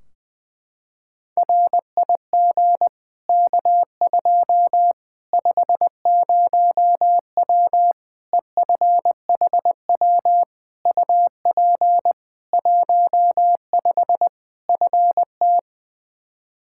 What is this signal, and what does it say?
Morse code, heard as a keyed tone, about 20 wpm, 700 Hz. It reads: RIG K2 50W EFHW UP 15 FT
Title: Practice QSO - Exchange 2 - Info - 20wpm